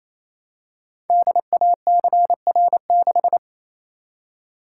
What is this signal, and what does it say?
Morse code, heard as a keyed tone, about 28 wpm, 700 Hz. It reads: DACR6
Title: Code Groups - 5 Letter:Numbers - 28wpm